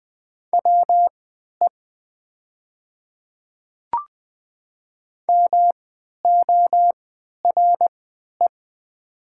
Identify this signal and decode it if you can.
Morse code, heard as a keyed tone, about 20 wpm, spaced as for 11 wpm, 700 Hz. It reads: WE MORE
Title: 3x Character Spacing - Top 100 Words - 20wpm